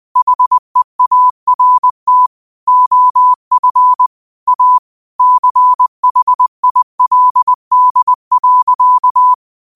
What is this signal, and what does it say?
Morse code, heard as a keyed tone, about 20 wpm, 1000 Hz. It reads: HEART OF A CHILD.